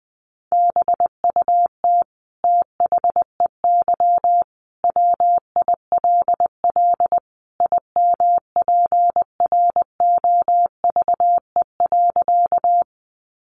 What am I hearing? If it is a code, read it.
Morse code, heard as a keyed tone, about 20 wpm, 700 Hz. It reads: BUT THEY WILL IMPROVE.
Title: Sentences from Top 800 Words 20wpm